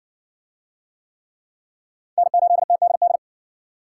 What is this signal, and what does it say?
Morse code, heard as a keyed tone, about 60 wpm, 700 Hz. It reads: N8TDD